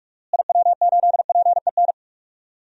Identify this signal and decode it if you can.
Morse code, heard as a keyed tone, about 44 wpm, 700 Hz. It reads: IW8PEN
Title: DXCC-100 Call Signs - 6 Characters - Repeated 2x - 45wpm